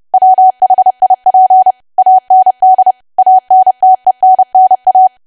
Morse code, heard as a keyed tone, about 30 wpm, 750 Hz. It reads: WHIP AND ANTENNA
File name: cwa / W1AW-Level3 / 30WPM